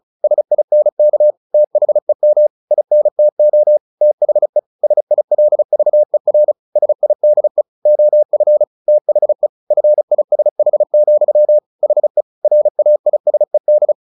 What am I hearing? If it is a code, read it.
Morse code, heard as a keyed tone, about 34 wpm, 600 Hz. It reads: SINK THEM INTO THE SILVER SIDE OF THE FISH, HE RAISED
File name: the old man and the sea 2 35wpm